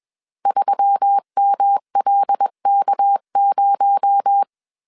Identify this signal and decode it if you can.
Morse code, heard as a keyed tone, about 21 wpm, 800 Hz. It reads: 3MLK0
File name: CW / DET / 1 6 1500C S A 20wpm 2048